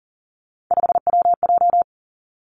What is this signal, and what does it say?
Morse code, heard as a keyed tone, about 38 wpm, 700 Hz. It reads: 5WJ
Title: Three Challenging Letter-Number Combinations 40wpm